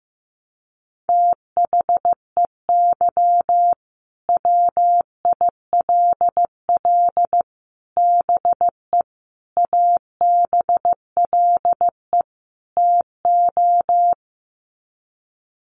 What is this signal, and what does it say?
Morse code, heard as a keyed tone, about 15 wpm, 700 Hz. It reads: THEY WILL BE ABLE TO
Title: Sets of 5 Words Encore 15wpm